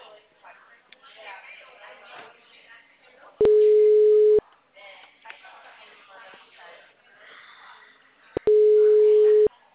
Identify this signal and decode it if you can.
Ringback tone (Europe, 425 Hz), with speech.